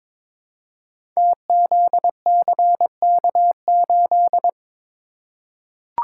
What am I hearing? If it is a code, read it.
Morse code, heard as a keyed tone, about 22 wpm, 700 Hz. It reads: TZCK8
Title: Code Groups - 5 Letter:Numbers - 22wpm